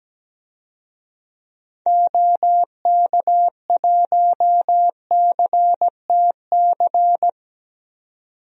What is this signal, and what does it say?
Morse code, heard as a keyed tone, about 17 wpm, 700 Hz. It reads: OK1CTC
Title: DXCC-100 Call Signs - 6 Characters - 17wpm